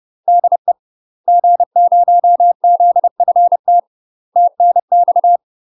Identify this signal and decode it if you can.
Morse code, heard as a keyed tone, about 30 wpm, 700 Hz. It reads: DE G0ZFT TNX
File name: UK MASTER QSOs TRAINING / 30WPM / UK MASTER QSOs 0026